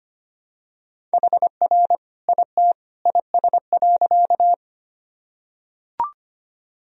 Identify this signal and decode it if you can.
Morse code, heard as a keyed tone, about 25 wpm, 700 Hz. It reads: HR IT IS.
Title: Sentences from Top 200 Words and Abbreviations 25wpm